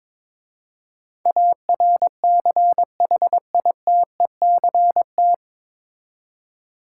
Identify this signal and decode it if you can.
Morse code, heard as a keyed tone, about 22 wpm, 700 Hz. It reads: ARCHITECT
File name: 9 Letter Words - First 3 Letters Given - 22wpm